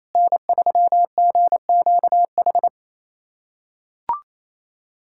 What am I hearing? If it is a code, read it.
Morse code, heard as a keyed tone, about 28 wpm, 700 Hz. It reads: N3GQH
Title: USA Callsign DE Callsign - 5 Characters - 28wpm